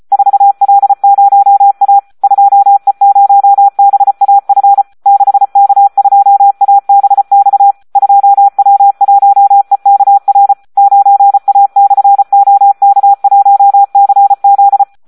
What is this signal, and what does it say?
Morse code, heard as a keyed tone, about 34 wpm, 800 Hz, with radio noise. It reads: 4L0A 2E0BAF 6K2ABX 2W1EKR 9A/OK1CZ
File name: Audio practica CW / SD 1 34